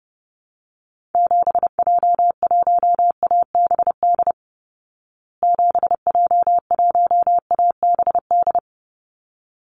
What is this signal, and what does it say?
Morse code, heard as a keyed tone, about 30 wpm, 700 Hz. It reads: 7J1ABD 7J1ABD
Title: Call Signs Repeated 3x - 30wpm